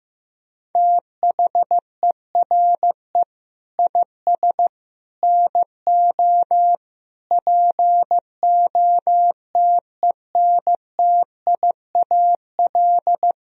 Morse code, heard as a keyed tone, about 15 wpm, 700 Hz. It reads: THERE IS NO POTENTIAL
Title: Sentences from Top 1000 Words - Review Part 4 - 15wpm